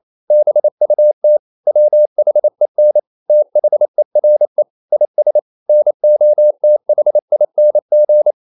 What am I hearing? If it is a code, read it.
Morse code, heard as a keyed tone, about 28 wpm, 600 Hz. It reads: BUT WHEN THERE IS NOTHING
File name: CW / Practice / EnglishQuotations0228